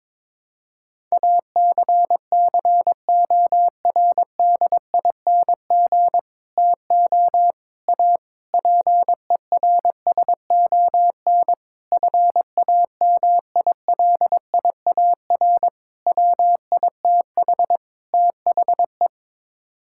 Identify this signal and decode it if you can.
Morse code, heard as a keyed tone, about 22 wpm, 700 Hz. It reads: ACCORDING TO A PERSON FAMILIAR WITH THE
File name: practice / 22wpm / Sets of 7 Words - 22wpm